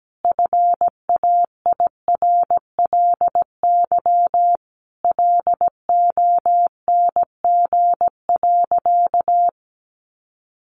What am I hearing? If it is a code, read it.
Morse code, heard as a keyed tone, about 17 wpm, 700 Hz. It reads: FAIRLY LONG.
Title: Sentences from Top 1100 Words 17wpm